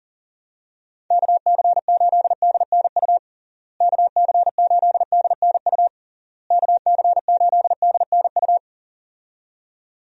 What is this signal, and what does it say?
Morse code, heard as a keyed tone, about 40 wpm, 700 Hz. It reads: KC8DNU KC8DNU KC8DNU
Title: USA Call Signs - 6 Characters - Repeated 3x - 40wpm